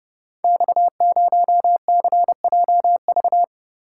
Morse code, heard as a keyed tone, about 30 wpm, 700 Hz. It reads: X0CJV
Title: Code Groups - 5 Letter:Numbers - 30wpm